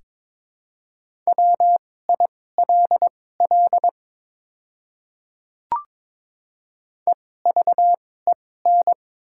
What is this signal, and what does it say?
Morse code, heard as a keyed tone, about 22 wpm, spaced as for 16 wpm, 700 Hz. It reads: WILL EVEN
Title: 2x Character Spacing - Top 300 Words - 22wpm